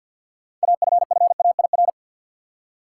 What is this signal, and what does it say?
Morse code, heard as a keyed tone, about 47 wpm, 700 Hz. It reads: AFFAIR
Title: Six Letter Words - 50wpm